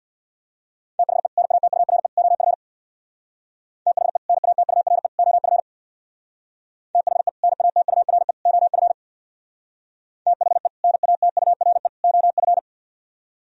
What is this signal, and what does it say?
Morse code, heard as a keyed tone, about 64 wpm, 700 Hz. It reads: THE NATURE OF THE NATURE OF THE NATURE OF THE NATURE OF